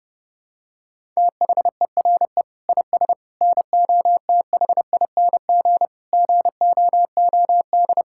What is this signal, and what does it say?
Morse code, heard as a keyed tone, about 28 wpm, 700 Hz. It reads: THERE IS NOTHING GOOD
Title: Sentences from Top 1000 Words - Review Part 5 - 30wpm